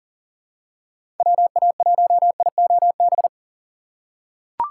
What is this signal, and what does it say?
Morse code, heard as a keyed tone, about 40 wpm, 700 Hz. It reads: WA1IOB
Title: USA Call Signs - 6 Characters - 4x Character Spacing - 40wpm